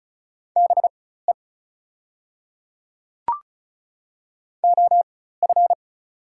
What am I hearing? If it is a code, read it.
Morse code, heard as a keyed tone, about 34 wpm, spaced as for 16 wpm, 700 Hz. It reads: BE OF
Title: 4x Character Spacing - Top 100 Words - 35wpm